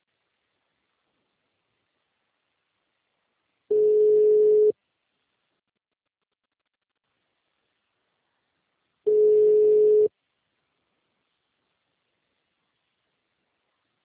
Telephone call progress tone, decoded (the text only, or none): ringback tone